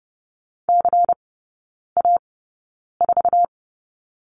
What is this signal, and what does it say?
Morse code, heard as a keyed tone, about 30 wpm, spaced as for 9 wpm, 690 Hz. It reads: CA4
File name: Lesson 16e - Letter c - Three Characters 30wpm